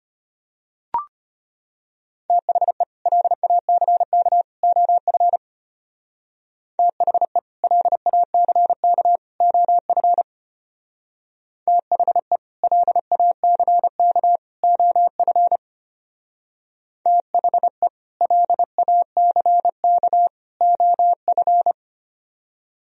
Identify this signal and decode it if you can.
Morse code, heard as a keyed tone, about 31 wpm, 700 Hz. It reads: THE LACK OF THE LACK OF THE LACK OF THE LACK OF
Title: Speed-Racing - Sets of 3 Words 25wpm